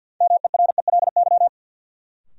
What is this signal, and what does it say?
Morse code, heard as a keyed tone, about 53 wpm, 700 Hz. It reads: MERELY
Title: Sentences from Top 1400 Words - New Word Drill Alt - 50wpm